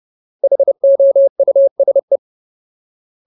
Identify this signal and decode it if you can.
Morse code, heard as a keyed tone, about 30 wpm, 550 Hz. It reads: HOUSE
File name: Ultimate Word List / 30WPM Ultimate